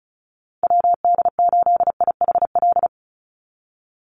Morse code, heard as a keyed tone, about 34 wpm, 700 Hz. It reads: WD8IHL
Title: USA Call Signs 35wpm